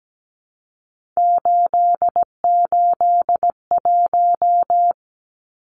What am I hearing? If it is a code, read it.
Morse code, heard as a keyed tone, about 17 wpm, 700 Hz. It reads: 881
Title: Numbers - 3-Digit 17wpm